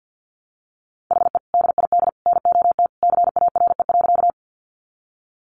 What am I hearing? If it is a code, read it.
Morse code, heard as a keyed tone, about 49 wpm, 700 Hz. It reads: HE DID NOT CARE.
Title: Sentences from Top 600 Words 50wpm